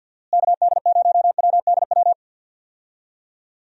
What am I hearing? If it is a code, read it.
Morse code, heard as a keyed tone, about 50 wpm, 700 Hz. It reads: KD0WDW
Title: USA Call Signs - 6 Characters - Repeated 3x - 50wpm